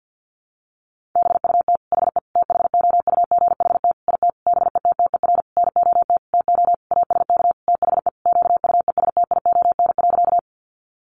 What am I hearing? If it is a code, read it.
Morse code, heard as a keyed tone, about 50 wpm, 700 Hz. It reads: BUT HE THOUGHT IT BETTER NOT TO ASK THE QUESTION.
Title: Sentences from Top 600 Words 50wpm